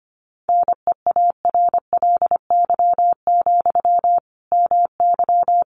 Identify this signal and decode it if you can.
Morse code, heard as a keyed tone, about 25 wpm, 700 Hz. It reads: NEARLY, MY
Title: Sentences from Top 1300 Words 25wpm